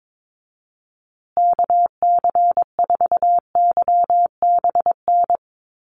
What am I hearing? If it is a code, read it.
Morse code, heard as a keyed tone, about 22 wpm, 700 Hz. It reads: KC4YBN
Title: USA Call Signs Repeated 3x 22wpm